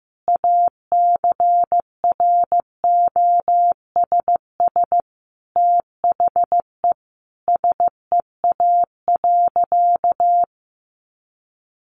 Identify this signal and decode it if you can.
Morse code, heard as a keyed tone, about 15 wpm, 700 Hz. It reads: ACROSS THE SEA.
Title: Sentences from Top 1300 Words 15wpm